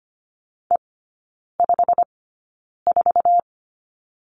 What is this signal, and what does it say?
Morse code, heard as a keyed tone, about 25 wpm, spaced as for 9 wpm, 690 Hz. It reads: E54
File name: Lesson 15e - Number 5 - Three Characters 25wpm